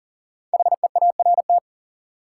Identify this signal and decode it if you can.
Morse code, heard as a keyed tone, about 40 wpm, 700 Hz. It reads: HEART
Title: Five Letter Words - 40wpm